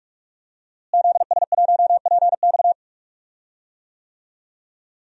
Morse code, heard as a keyed tone, about 46 wpm, 700 Hz. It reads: ZS1PX
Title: DXCC-100 Call Signs - 5 Characters - Repeated 2x - 45wpm